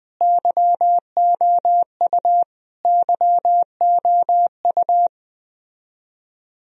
Morse code, heard as a keyed tone, about 20 wpm, 700 Hz. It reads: YOU YOU
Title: Top 100 Words Repeated 20wpm